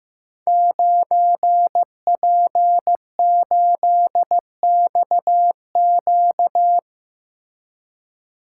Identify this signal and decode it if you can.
Morse code, heard as a keyed tone, about 15 wpm, 700 Hz. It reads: 9P8XQ
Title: Code Groups - 5 Letter:Numbers - 15wpm